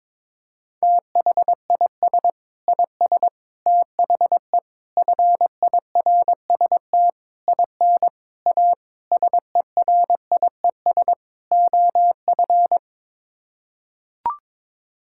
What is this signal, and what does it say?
Morse code, heard as a keyed tone, about 22 wpm, 700 Hz. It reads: THIS IS THE FIRST IN A SERIES OF
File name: practice / 22wpm / Sets of 8 Words - 22wpm